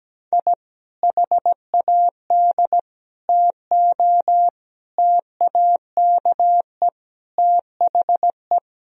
Morse code, heard as a keyed tone, about 17 wpm, 700 Hz. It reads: I HAD TO TAKE THE
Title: Sentences from Top 1000 Words - Review Part 4 - 17wpm